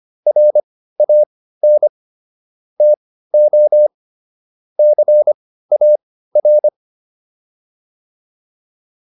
Morse code, heard as a keyed tone, about 25 wpm, spaced as for 15 wpm, 600 Hz. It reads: RAN TO CAR